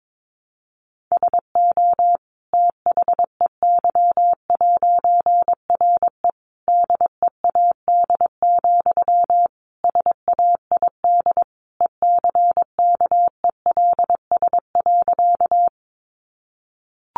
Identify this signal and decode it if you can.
Morse code, heard as a keyed tone, about 22 wpm, 700 Hz. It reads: SO THEY'RE DEAD, SAID ECKELS.
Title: A Sound Of Thunder 22wpm